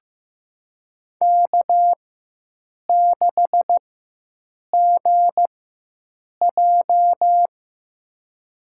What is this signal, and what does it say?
Morse code, heard as a keyed tone, about 15 wpm, spaced as for 7 wpm, 700 Hz. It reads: K6GJ
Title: USA Call Signs - 4x Character Spacing - 15wpm